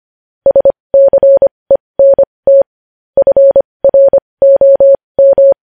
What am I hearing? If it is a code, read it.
Morse code, heard as a keyed tone, about 25 wpm, spaced as for 20 wpm, 550 Hz. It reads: SCENT FROM
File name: podcast / News headlines-1x-25x20